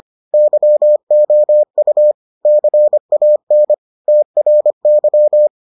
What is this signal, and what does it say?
Morse code, heard as a keyed tone, about 25 wpm, 600 Hz. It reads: YOU CAN TRY